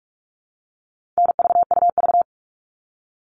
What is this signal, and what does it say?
Morse code, heard as a keyed tone, about 44 wpm, 700 Hz. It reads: NVUV
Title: All Cut Numbers - 4-Digit 45wpm